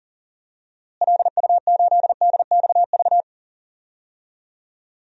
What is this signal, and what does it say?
Morse code, heard as a keyed tone, about 40 wpm, 700 Hz. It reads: LU8DXV
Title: DXCC-100 Call Signs - 6 Characters - Repeated 2x - 40wpm